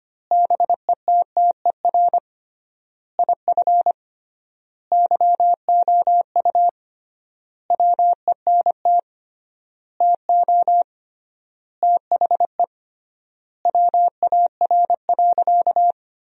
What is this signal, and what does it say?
Morse code, heard as a keyed tone, about 25 wpm, 700 Hz. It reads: BETTER IF YOU WENT TO THE WAR.
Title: 3x Word-Spacing - Sentences from Top 400 Words - 25wpm